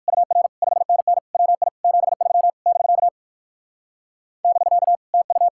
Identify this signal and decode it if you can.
Morse code, heard as a keyed tone, about 61 wpm, 700 Hz. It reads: UR 5NN WI 73 <BK> <BK> TU